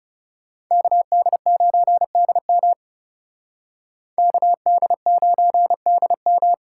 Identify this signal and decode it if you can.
Morse code, heard as a keyed tone, about 38 wpm, 700 Hz. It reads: KD9DM KD9DM